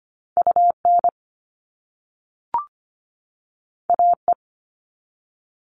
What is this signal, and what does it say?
Morse code, heard as a keyed tone, about 25 wpm, 700 Hz. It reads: UN AE
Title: All Cut Numbers - 2-Digit 25wpm